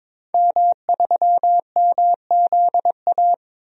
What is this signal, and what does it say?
Morse code, heard as a keyed tone, about 21 wpm, 700 Hz. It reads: M3MZA
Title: DXCC-100 Call Signs - 5 Characters - 22wpm